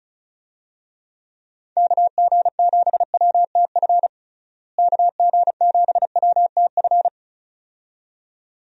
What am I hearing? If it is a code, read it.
Morse code, heard as a keyed tone, about 34 wpm, 700 Hz. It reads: KG7WTF KG7WTF
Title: USA Call Signs - 6 Characters - Repeated 2x - 35wpm